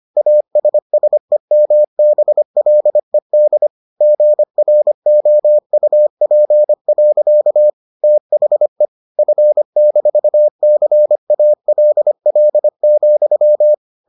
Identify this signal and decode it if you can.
Morse code, heard as a keyed tone, about 25 wpm, 600 Hz. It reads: ASSEMBLED GROUP. THE F-CALL,